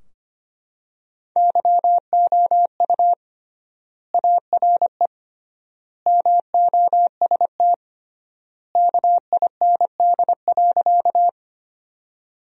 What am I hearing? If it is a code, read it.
Morse code, heard as a keyed tone, about 25 wpm, 700 Hz. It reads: YOU ARE MOST KIND.
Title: 3x Word-Spacing - Sentences from Top 200 Words - 25wpm